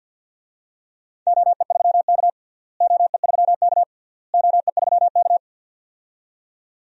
Morse code, heard as a keyed tone, about 52 wpm, 700 Hz. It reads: OE3K OE3K OE3K
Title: DXCC-100 Call Signs - 4 Characters - Repeated 3x - 50wpm